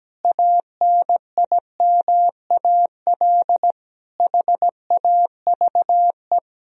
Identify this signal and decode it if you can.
Morse code, heard as a keyed tone, about 17 wpm, 700 Hz. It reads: ANIMAL HAVE
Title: Sentences from Top 1000 Words - Review Part 1 - 17wpm